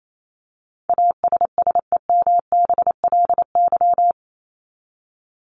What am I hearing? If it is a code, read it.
Morse code, heard as a keyed tone, about 28 wpm, 700 Hz. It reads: ASSEMBLY
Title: LY Suffix - Top 100 Words with Base Words 28wpm